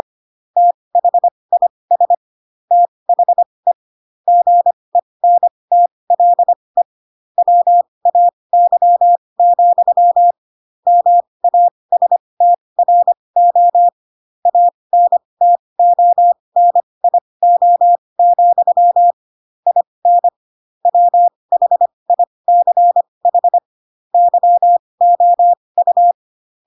Morse code, heard as a keyed tone, about 25 wpm, spaced as for 20 wpm, 700 Hz. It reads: THIS THE GENTLE WAY, MASTRO ANTONIO, IN WHICH YOU
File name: Pinocchio / Pinocchio ch2 15 25 ik1hge